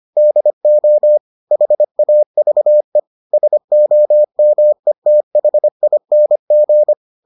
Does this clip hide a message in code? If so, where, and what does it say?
Yes: Morse code from 0.2 to 6.9 s, heard as a keyed tone, about 25 wpm, 600 Hz. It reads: DO HAVE SOMETHING